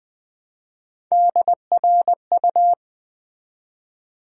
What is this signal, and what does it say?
Morse code, heard as a keyed tone, about 20 wpm, 700 Hz. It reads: DRU
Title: Build A Word - 1 Letter At A Time - 3 to 5-Letter Words - 20wpm